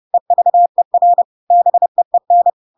Morse code, heard as a keyed tone, about 30 wpm, 700 Hz. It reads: EVER BEEN
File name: Pinocchio / Pinocchio ch3 30 30 ik1hge